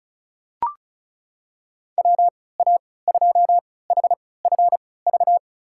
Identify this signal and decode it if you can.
Morse code, heard as a keyed tone, about 36 wpm, spaced as for 20 wpm, 700 Hz. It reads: WA2HFV
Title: USA Call Signs - 6 Characters - 3x Character Spacing - 35wpm